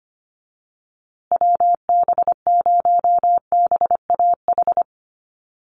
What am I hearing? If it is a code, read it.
Morse code, heard as a keyed tone, about 25 wpm, 700 Hz. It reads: WB0BAH
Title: USA Call Signs Repeated 3x 25wpm